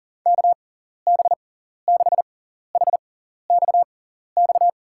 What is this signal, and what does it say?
Morse code, heard as a keyed tone, about 37 wpm, spaced as for 14 wpm, 700 Hz. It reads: KB6HXX
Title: USA Call Signs - 6 Characters - 6x Character Spacing - 40wpm